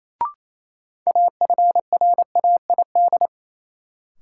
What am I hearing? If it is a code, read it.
Morse code, heard as a keyed tone, about 27 wpm, 700 Hz. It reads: AFRAID
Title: Sentences from Top 1500 Words - New Word Drill Alt - 28wpm